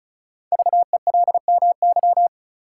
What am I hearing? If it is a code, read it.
Morse code, heard as a keyed tone, about 36 wpm, 700 Hz. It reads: VELMY
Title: Code Groups - 5 Letters - 35wpm